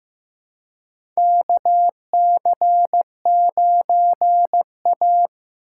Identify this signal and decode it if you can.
Morse code, heard as a keyed tone, about 15 wpm, 700 Hz. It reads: KC9A
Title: USA Call Signs - 4 Characters - 3x Character Spacing - 15wpm